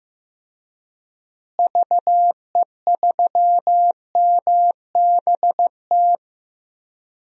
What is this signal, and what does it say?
Morse code, heard as a keyed tone, about 15 wpm, 700 Hz. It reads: VE3MBT
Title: DXCC-100 Call Signs - 6 Characters - Repeated 3x - 15wpm